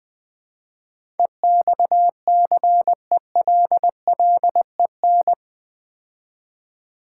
Morse code, heard as a keyed tone, about 20 wpm, 700 Hz. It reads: EXCELLEN
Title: Build A Word - 4 Letters At A Time - 10-Letter Words - 20wpm